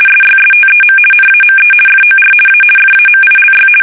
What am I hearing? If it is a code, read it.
RTTY, about 50 baud. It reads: SYSTEM NAME FOR A FAMILY
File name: fsk850-50B / teletype-corp